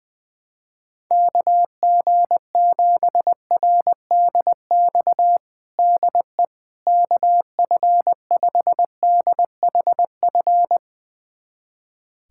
Morse code, heard as a keyed tone, about 20 wpm, 700 Hz. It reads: KG7RDX DE KF5DHF
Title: USA Callsign DE Callsign - 6 Characters - 20wpm